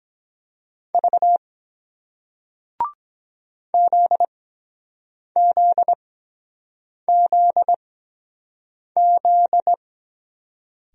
Morse code, heard as a keyed tone, about 23 wpm, spaced as for 7 wpm, 700 Hz. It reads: V ZZZZ